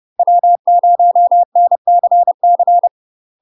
Morse code, heard as a keyed tone, about 30 wpm, 700 Hz. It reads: W0NCC
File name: UK MASTER QSOs TRAINING / 30WPM / UK MASTER QSOs 0016